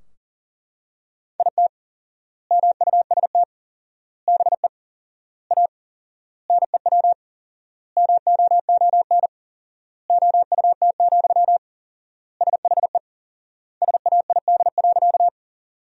Morse code, heard as a keyed tone, about 40 wpm, 700 Hz. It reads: IT MUST BE A NEW MOON OUT, SHE SAID.
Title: 4x Word-Spacing - Sentences from Top 500 Words - 40wpm